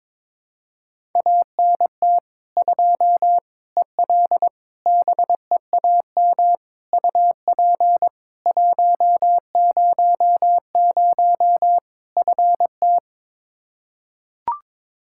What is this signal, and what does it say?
Morse code, heard as a keyed tone, about 22 wpm, 700 Hz. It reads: ANT 2 EL BEAM UP 100 FT
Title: K1USN SST QSO Practice 22wpm